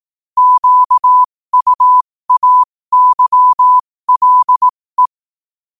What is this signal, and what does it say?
Morse code, heard as a keyed tone, about 18 wpm, spaced as for 15 wpm, 1000 Hz. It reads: QUAYLE
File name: qotdfiles / 15wpm